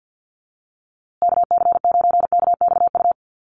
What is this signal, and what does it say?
Morse code, heard as a keyed tone, about 49 wpm, 700 Hz. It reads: KC9KXU
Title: USA Call Signs Repeated 3x 50wpm